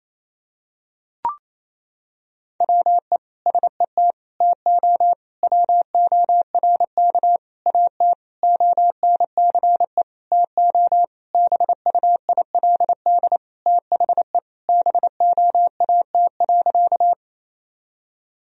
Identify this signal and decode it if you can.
Morse code, heard as a keyed tone, about 28 wpm, 700 Hz. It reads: WE SET TO WORK AT ONCE TO BUILD THE BOAT.
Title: Sentences from Top 1000 Words - Review Part 1 - 28wpm